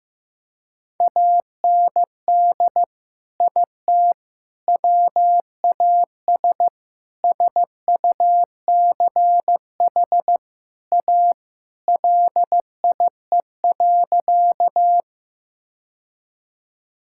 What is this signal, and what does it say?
Morse code, heard as a keyed tone, about 15 wpm, 700 Hz. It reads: AND IT WAS SUCH A LIE.
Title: Sentences from Top 1600 Words 15wpm